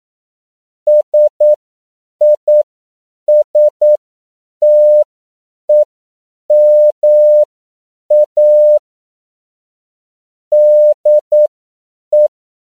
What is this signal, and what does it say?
Morse code, heard as a keyed tone, about 9 wpm, spaced as for 7 wpm, 600 Hz. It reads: SISTEMA DE
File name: CW / 10PPM